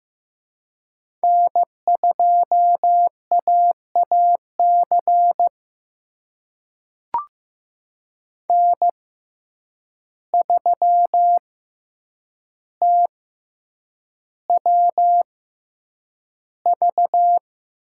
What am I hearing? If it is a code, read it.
Morse code, heard as a keyed tone, about 15 wpm, spaced as for 8 wpm, 700 Hz. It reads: N2AAC N3TWV
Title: USA Call Signs - 5 Characters - 6x Character Spacing - 15wpm